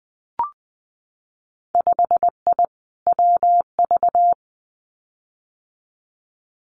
Morse code, heard as a keyed tone, about 19 wpm, 700 Hz. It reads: 5I WV
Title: Winter Field Day Exchanges Encore 20wpm